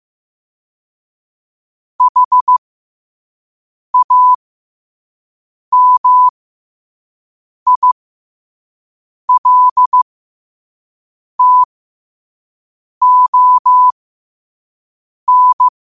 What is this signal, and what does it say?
Morse code, heard as a keyed tone, about 15 wpm, spaced as for 5 wpm, 1000 Hz. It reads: HAMILTON
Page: New quotes every day in morse code at 5 Words per minute.